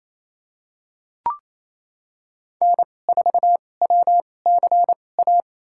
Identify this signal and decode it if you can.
Morse code, heard as a keyed tone, about 29 wpm, spaced as for 21 wpm, 700 Hz. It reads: N4WCA
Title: USA Call Signs - 5 Characters - 2x Character Spacing - 28wpm